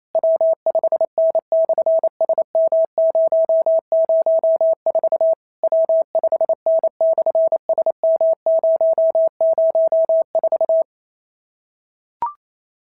Morse code, heard as a keyed tone, about 28 wpm, 690 Hz. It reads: W5N/SM004 W5N/SM004
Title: SOTA QSO Practice 28wpm